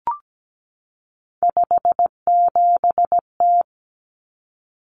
Morse code, heard as a keyed tone, about 17 wpm, 700 Hz. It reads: 57T
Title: Common Cut Numbers - 3-Digit 17wpm